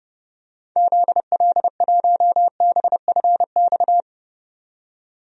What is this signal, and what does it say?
Morse code, heard as a keyed tone, about 30 wpm, 700 Hz. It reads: ZL1BFX
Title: DXCC-100 Call Signs - 6 Characters - 30wpm